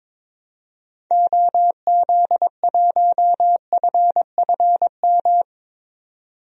Morse code, heard as a keyed tone, about 22 wpm, 700 Hz. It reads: OZ1FFM
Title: DXCC-100 Call Signs - 6 Characters - 22wpm